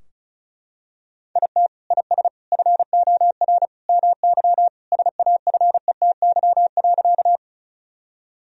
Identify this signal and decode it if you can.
Morse code, heard as a keyed tone, about 36 wpm, 700 Hz. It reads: IT IS FOR MY SAFETY.
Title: Sentences from Top 1000 Words - Review Part 4 - 35wpm